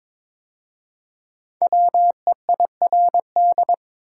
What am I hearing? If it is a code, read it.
Morse code, heard as a keyed tone, about 23 wpm, 690 Hz. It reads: WEIRD
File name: 5 Letter Words - First 3 Letters Given - 22wpm